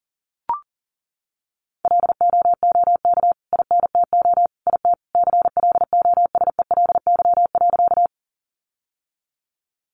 Morse code, heard as a keyed tone, about 40 wpm, 700 Hz. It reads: LOOK INTO IT CLOSELY.
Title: Sentences from Top 1400 Words 40wpm